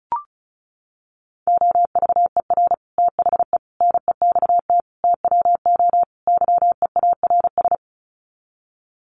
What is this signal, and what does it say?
Morse code, heard as a keyed tone, about 34 wpm, 700 Hz. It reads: OVER THE NEXT TWO YEARS
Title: Sets of 5 Words Encore2 35wpm